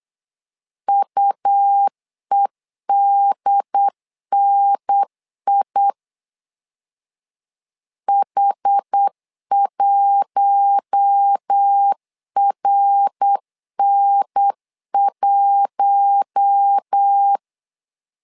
Morse code, heard as a keyed tone, about 8 wpm, 800 Hz. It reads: UEDNI H1RN1
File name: LEZIONI / 4 Q1 8wpm 3035